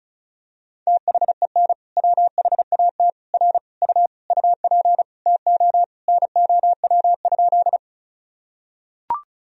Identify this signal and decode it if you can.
Morse code, heard as a keyed tone, about 36 wpm, 700 Hz. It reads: THEN WHAT R U UP TO NOW?
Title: Sentences from Top 100 Words and Abbreviations 35wpm